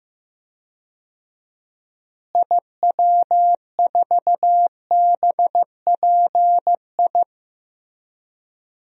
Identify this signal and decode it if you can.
Morse code, heard as a keyed tone, about 15 wpm, 700 Hz. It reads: IW4BPI
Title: DXCC-100 Call Signs - 6 Characters - Repeated 2x - 15wpm